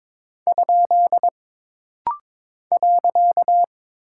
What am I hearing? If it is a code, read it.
Morse code, heard as a keyed tone, about 22 wpm, 700 Hz. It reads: ? .
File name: practice / 22wpm / Single Punctuation-Prosign - Rapid-Fire - 22wpm